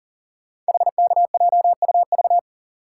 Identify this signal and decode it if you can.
Morse code, heard as a keyed tone, about 40 wpm, 700 Hz. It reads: HKJUV
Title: Code Groups - 5 Letters - 40wpm